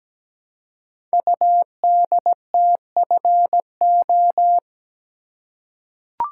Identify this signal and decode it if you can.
Morse code, heard as a keyed tone, about 17 wpm, 700 Hz. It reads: UDTFO
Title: Code Groups - 5 Letters - 17wpm